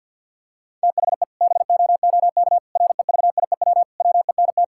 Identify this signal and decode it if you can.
Morse code, heard as a keyed tone, about 52 wpm, 700 Hz. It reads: THE BOOK REVIEW WENT
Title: Sentences from Top 1000 Words - Review Part 6 - 50wpm